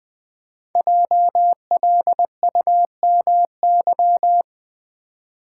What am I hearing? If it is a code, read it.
Morse code, heard as a keyed tone, about 20 wpm, 700 Hz. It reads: JLUMY
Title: Code Groups - 5 Letters - 20wpm